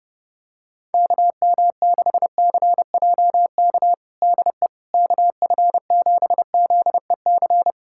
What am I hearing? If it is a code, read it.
Morse code, heard as a keyed tone, about 30 wpm, 700 Hz. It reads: KM6CJK DE KF7ZEC
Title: USA Callsign DE Callsign - 6 Characters - 30wpm